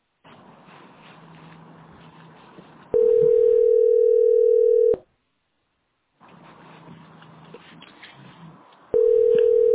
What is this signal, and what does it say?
Ringback tone (North America), 440 + 480 Hz.